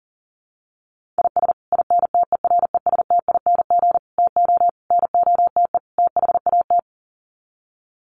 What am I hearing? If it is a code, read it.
Morse code, heard as a keyed tone, about 40 wpm, 700 Hz. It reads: IS INTERESTING TO NOTE THAT
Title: Sets of 5 Words Encore2 40wpm